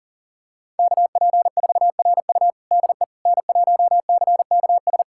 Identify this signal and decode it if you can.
Morse code, heard as a keyed tone, about 40 wpm, 700 Hz. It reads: KP4RU DE N1CKS
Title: USA Callsign DE Callsign - 5 Characters - 40wpm